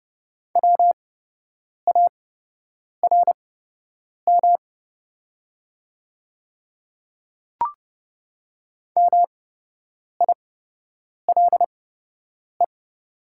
Morse code, heard as a keyed tone, about 30 wpm, spaced as for 8 wpm, 690 Hz. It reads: WARM MILE
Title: 8x Character Spacing - Top 500 Words - 30wpm